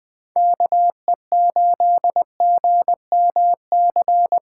Morse code, heard as a keyed tone, about 20 wpm, 700 Hz. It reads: KE8GMC
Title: USA Callsign DE Callsign - 6 Characters - 20wpm